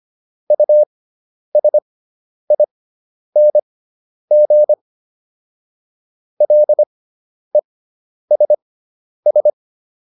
Morse code, heard as a keyed tone, about 25 wpm, spaced as for 10 wpm, 600 Hz. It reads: USING LESS